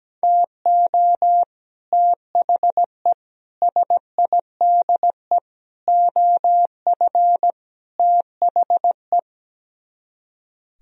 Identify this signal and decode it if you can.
Morse code, heard as a keyed tone, about 17 wpm, 700 Hz. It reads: TO THE SIDE OF THE
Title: Speed-Racing - Sets of 5 Words 17wpm